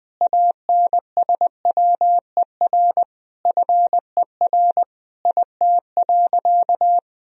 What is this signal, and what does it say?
Morse code, heard as a keyed tone, about 20 wpm, 700 Hz. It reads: ANSWER FER IT.
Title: Sentences from Top 300 Words and Abbreviations 20wpm